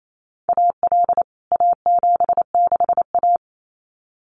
Morse code, heard as a keyed tone, about 28 wpm, 700 Hz. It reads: AL A76A
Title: CWT Exchange 28wpm